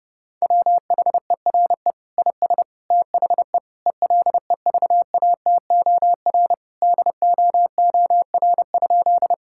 Morse code, heard as a keyed tone, about 30 wpm, 700 Hz. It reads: WHERE IS THE ELEVATOR DOOR?
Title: Sentences from Top 1600 Words 30wpm